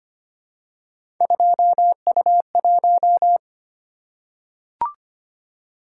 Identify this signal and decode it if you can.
Morse code, heard as a keyed tone, about 25 wpm, 700 Hz. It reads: 2U1
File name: Three Letter-Number Combinations - 25wpm